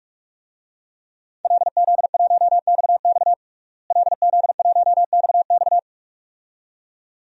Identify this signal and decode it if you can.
Morse code, heard as a keyed tone, about 46 wpm, 700 Hz. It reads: LZ1XX LZ1XX
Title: DXCC-100 Call Signs - 5 Characters - Repeated 2x - 45wpm